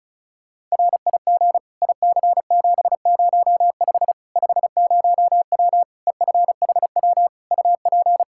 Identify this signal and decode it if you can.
Morse code, heard as a keyed tone, about 34 wpm, 700 Hz. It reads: RIG IC705 50W EFHW UP
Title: Practice QSO - Exchange 2 - Info - 35wpm